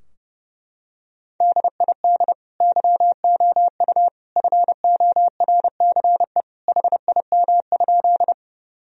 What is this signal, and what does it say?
Morse code, heard as a keyed tone, about 30 wpm, 700 Hz. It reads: DID YOU FORCE HIM?
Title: Sentences from Top 1000 Words - Review Part 3 - 30wpm